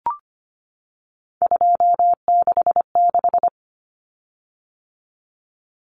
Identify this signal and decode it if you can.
Morse code, heard as a keyed tone, about 25 wpm, 700 Hz. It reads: 266
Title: Common Cut Numbers - 3-Digit 25wpm